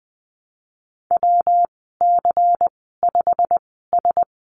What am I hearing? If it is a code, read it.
Morse code, heard as a keyed tone, about 20 wpm, spaced as for 15 wpm, 700 Hz. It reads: WC5S
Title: 140 Lesson 23g - Letter f - Call Signs 20wpm